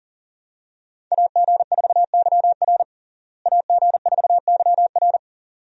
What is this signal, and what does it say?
Morse code, heard as a keyed tone, about 37 wpm, 700 Hz. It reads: AG4YR AG4YR
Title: USA Call Signs - 5 Characters - Repeated 2x - 40wpm